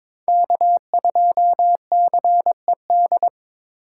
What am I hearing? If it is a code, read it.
Morse code, heard as a keyed tone, about 26 wpm, 700 Hz. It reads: K2CED
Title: Speed Racing - USA Call Signs 22wpm